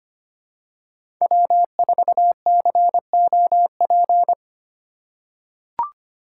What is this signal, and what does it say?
Morse code, heard as a keyed tone, about 25 wpm, 700 Hz. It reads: W4COP
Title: USA Call Signs - 5 Characters - 3x Character Spacing - 25wpm